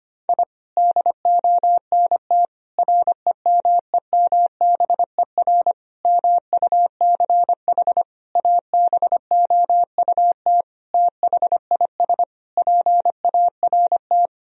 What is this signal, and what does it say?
Morse code, heard as a keyed tone, about 25 wpm, 700 Hz. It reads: I DONT REMEMBER MUCH ABOUT THIS PART
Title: Sentences from Top 1000 Words - Review Part 1 - 25wpm